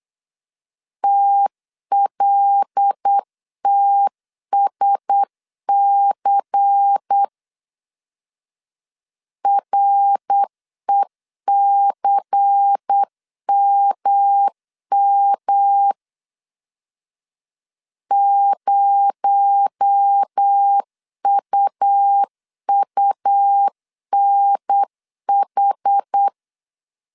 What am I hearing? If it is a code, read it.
Morse code, heard as a keyed tone, about 8 wpm, 800 Hz. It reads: TLTSC RECMM 0UUNH
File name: CW / DET / 1 6 1500C S C 8wpm 5219